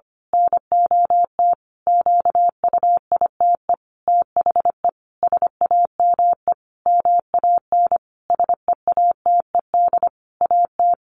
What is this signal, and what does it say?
Morse code, heard as a keyed tone, about 25 wpm, 700 Hz. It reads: NOT QUITE THE SAME MAN SEATED AT
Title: A Sound Of Thunder 25wpm